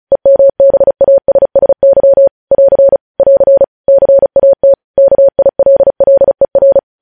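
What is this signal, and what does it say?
Morse code, heard as a keyed tone, about 36 wpm, 550 Hz. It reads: EMBASSY <AR> <AR> CAT KILLER
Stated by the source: News headlines sent in Morse at 35 WPM.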